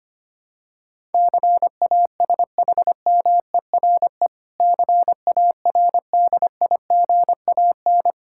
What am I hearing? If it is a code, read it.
Morse code, heard as a keyed tone, about 25 wpm, 700 Hz. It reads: CASHMERE CARDIGAN
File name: practice / 25wpm / Difficult Words with Hints - 25wpm